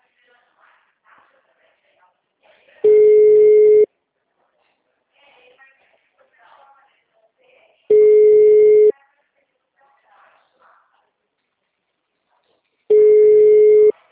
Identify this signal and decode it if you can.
Ringback tone (Europe, 425 Hz), with speech.